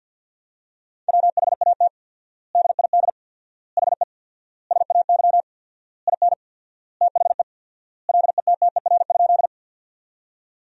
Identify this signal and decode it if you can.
Morse code, heard as a keyed tone, about 50 wpm, 700 Hz. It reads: WHAT DID HE SAY IN THE LETTER?
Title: 4x Word-Spacing - Sentences from Top 300 Words - 50wpm